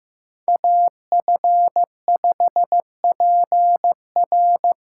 Morse code, heard as a keyed tone, about 15 wpm, 700 Hz. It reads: AF5PR
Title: USA Call Signs - 5 Characters - 3x Character Spacing - 15wpm